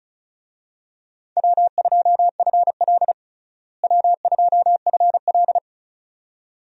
Morse code, heard as a keyed tone, about 34 wpm, 700 Hz. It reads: W2FL W2FL
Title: USA Call Signs - 4 Characters - Repeated 2x - 35wpm